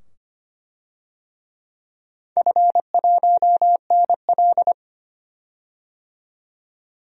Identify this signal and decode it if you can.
Morse code, heard as a keyed tone, about 25 wpm, 700 Hz. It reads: F1NL
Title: DXCC-100 Call Signs - 4 Characters - 25wpm